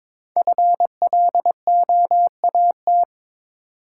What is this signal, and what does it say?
Morse code, heard as a keyed tone, about 22 wpm, 690 Hz. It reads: FLOAT
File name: 5 Letter Words - First 3 Letters Given - 22wpm